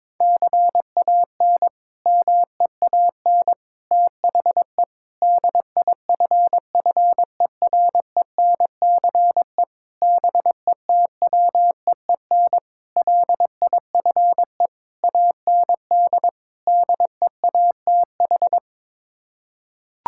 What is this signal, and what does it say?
Morse code, heard as a keyed tone, about 22 wpm, 700 Hz. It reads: CAN MEAN THE DIFFERENCE BETWEEN LIFE AND DEATH
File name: practice / 22wpm / Sets of 8 Words - 22wpm